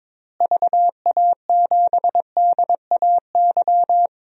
Morse code, heard as a keyed tone, about 22 wpm, 700 Hz. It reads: VA7DAY
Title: DXCC-100 Call Signs - 6 Characters - Repeated 3x - 22wpm